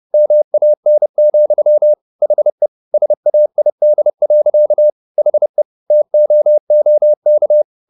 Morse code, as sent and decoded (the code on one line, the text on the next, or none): -- .- -. --..-- / .... / ... .- .. -.. .-.-.- / .... / - --- --- -.-
MAN, HE SAID. HE TOOK